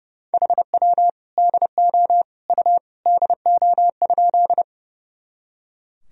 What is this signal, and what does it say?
Morse code, heard as a keyed tone, about 30 wpm, 700 Hz. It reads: HW DO U DO?
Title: Sentences from Top 100 Words and Abbreviations 30wpm